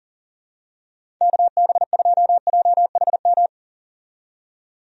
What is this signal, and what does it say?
Morse code, heard as a keyed tone, about 40 wpm, 700 Hz. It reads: KB2JHM
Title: USA Call Signs - 6 Characters - 40wpm